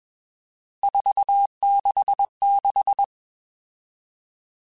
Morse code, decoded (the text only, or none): 466